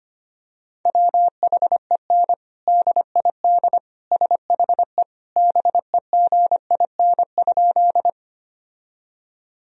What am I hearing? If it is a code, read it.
Morse code, heard as a keyed tone, about 25 wpm, 700 Hz. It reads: WHEN DID SHE BEGIN?
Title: Sentences from Top 1000 Words - Review Part 1 - 25wpm